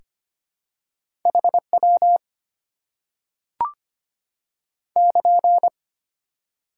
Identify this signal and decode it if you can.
Morse code, heard as a keyed tone, about 25 wpm, 700 Hz. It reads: HW <KN>
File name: Top 90 QSO Elements - 6x Character Spacing - 25wpm